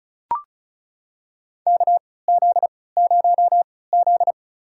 Morse code, heard as a keyed tone, about 36 wpm, spaced as for 20 wpm, 700 Hz. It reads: KZ0Z
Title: USA Call Signs - 4 Characters - 3x Character Spacing - 35wpm